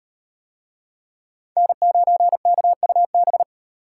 Morse code, heard as a keyed tone, about 38 wpm, 700 Hz. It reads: N9KUB